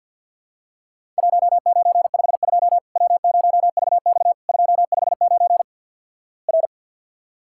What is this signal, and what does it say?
Morse code, heard as a keyed tone, about 50 wpm, 700 Hz. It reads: 1952 W0VX 259 R
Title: Worked All Europe DX Contest - QTC Exchange - 50wpm